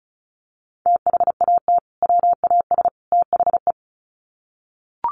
Morse code, heard as a keyed tone, about 34 wpm, 700 Hz. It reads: THAT WAS THE
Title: Sets of 3 Words 35wpm